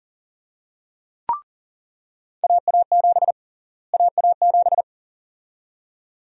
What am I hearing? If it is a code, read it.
Morse code, heard as a keyed tone, about 40 wpm, 700 Hz. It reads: AA7 AA7
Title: USA Call Sign Prefixes - Repeated 2x - 40wpm